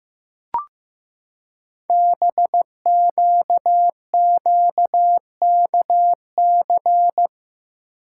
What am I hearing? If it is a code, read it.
Morse code, heard as a keyed tone, about 15 wpm, 700 Hz. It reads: BQQKC
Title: Code Groups - 5 Letters - 15wpm